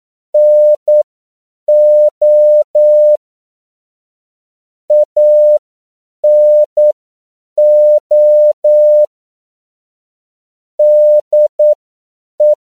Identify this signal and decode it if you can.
Morse code, heard as a keyed tone, about 9 wpm, spaced as for 7 wpm, 600 Hz. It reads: NO ANO DE
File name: CW / 10PPM